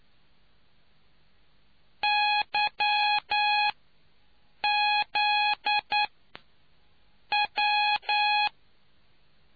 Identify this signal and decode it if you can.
Morse code, heard as a keyed tone, about 9 wpm, spaced as for 6 wpm, 830 Hz. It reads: YZ W